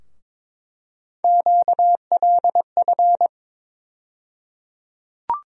Morse code, heard as a keyed tone, about 22 wpm, 700 Hz. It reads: QLF
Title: USA Call Sign Postfixes - 3 Letters 22wpm